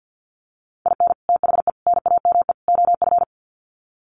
Morse code, heard as a keyed tone, about 50 wpm, 700 Hz. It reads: IN THE NAME OF
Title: Sets of 4 Words 50wpm